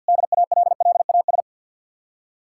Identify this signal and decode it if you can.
Morse code, heard as a keyed tone, about 67 wpm, spaced as for 58 wpm, 700 Hz. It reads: DALLAS
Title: Speed-Racing Common QSO Phrases - 50wpm